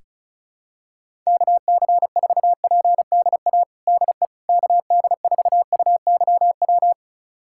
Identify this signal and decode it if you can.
Morse code, heard as a keyed tone, about 34 wpm, 700 Hz. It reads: KC4PDA DE KD4UYW
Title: USA Callsign DE Callsign - 6 Characters - 35wpm